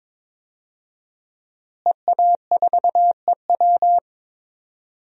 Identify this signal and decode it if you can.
Morse code, heard as a keyed tone, about 22 wpm, 700 Hz. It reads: EA4EW
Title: DXCC-100 Call Signs - 5 Characters - Repeated 2x - 22wpm